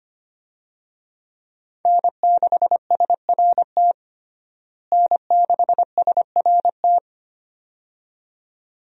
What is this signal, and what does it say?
Morse code, heard as a keyed tone, about 25 wpm, 700 Hz. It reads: N6SRT N6SRT
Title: USA Call Signs - 5 Characters - Repeated 2x - 25wpm